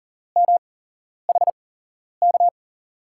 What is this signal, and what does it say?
Morse code, heard as a keyed tone, about 40 wpm, spaced as for 11 wpm, 700 Hz. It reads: MHK
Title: USA Call Signs - 6 Characters - 8x Character Spacing - 40wpm